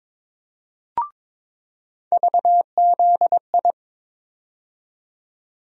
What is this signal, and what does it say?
Morse code, heard as a keyed tone, about 23 wpm, 700 Hz. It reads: VZI
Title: USA Call Sign Postfixes - 3 Letters 22wpm